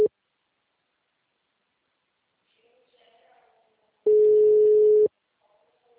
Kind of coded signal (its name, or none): telephone call progress tone